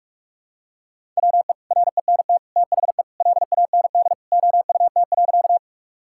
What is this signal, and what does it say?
Morse code, heard as a keyed tone, about 45 wpm, 700 Hz. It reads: WE RENT THE LAND OUT.
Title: Sentences from Top 1000 Words - Review Part 5 - 45wpm